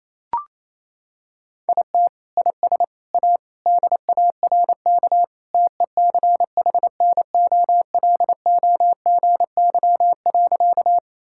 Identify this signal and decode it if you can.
Morse code, heard as a keyed tone, about 28 wpm, 700 Hz. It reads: IT IS A DARK TECHNOLOGY.
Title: Sentences from Top 1000 Words - Review Part 5 - 28wpm